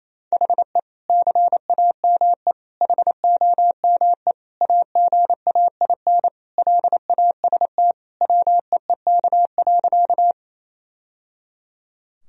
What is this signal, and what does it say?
Morse code, heard as a keyed tone, about 28 wpm, 700 Hz. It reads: HE CAME HOME AGAIN LAST WEEK.
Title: Sentences from Top 1000 Words - Review Part 2 - 28wpm